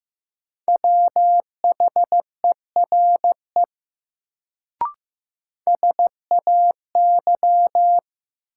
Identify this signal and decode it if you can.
Morse code, heard as a keyed tone, about 15 wpm, 700 Hz. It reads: WHERE SAY
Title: Sets of 2 Words - New Word Drill - 15wpm